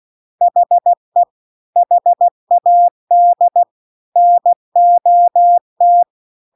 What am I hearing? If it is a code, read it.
Morse code, heard as a keyed tone, about 16 wpm, 700 Hz. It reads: HE HAD NOT